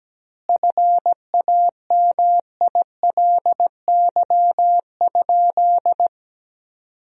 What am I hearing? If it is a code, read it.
Morse code, heard as a keyed tone, about 17 wpm, 700 Hz. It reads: FAMILY?
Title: 2x Word-Spacing - Sentences from Top 400 Words - 17wpm